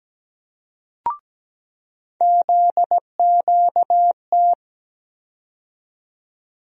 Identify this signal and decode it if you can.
Morse code, heard as a keyed tone, about 17 wpm, 700 Hz. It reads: ZQT
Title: USA Call Sign Postfixes - 3 Letters 17wpm